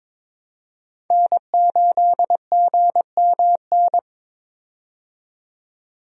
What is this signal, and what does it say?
Morse code, heard as a keyed tone, about 21 wpm, 700 Hz. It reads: N8GMN
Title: USA Call Signs - 5 Characters - 22wpm